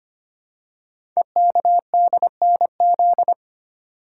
Morse code, heard as a keyed tone, about 25 wpm, 700 Hz. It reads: EKDNZ
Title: Code Groups - 5 Letters - 25wpm